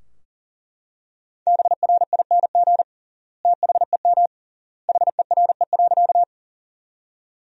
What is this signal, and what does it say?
Morse code, heard as a keyed tone, about 40 wpm, 700 Hz. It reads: BRING THEM HERE.
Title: 3x Word-Spacing - Sentences from Top 500 Words - 40wpm